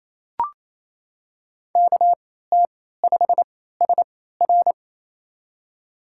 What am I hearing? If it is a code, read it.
Morse code, heard as a keyed tone, about 28 wpm, spaced as for 16 wpm, 700 Hz. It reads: KT5SR
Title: USA Call Signs - 3x Character Spacing - 28wpm